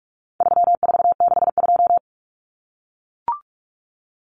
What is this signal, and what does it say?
Morse code, heard as a keyed tone, about 44 wpm, 700 Hz. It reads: 3462
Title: Common Cut Numbers - 4-Digit 45wpm